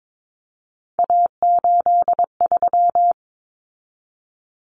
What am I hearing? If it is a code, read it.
Morse code, heard as a keyed tone, about 22 wpm, 700 Hz. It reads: A83
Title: Common Cut Numbers - 3-Digit 22wpm